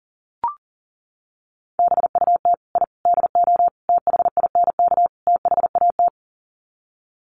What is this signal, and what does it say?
Morse code, heard as a keyed tone, about 40 wpm, 700 Hz. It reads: BUT I DO THINK THAT
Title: Sets of 5 Words Encore2 40wpm